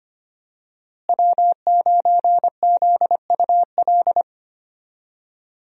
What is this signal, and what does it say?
Morse code, heard as a keyed tone, about 25 wpm, 700 Hz. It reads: W9ZUL
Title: USA Call Signs - 5 Characters - Repeated 3x - 25wpm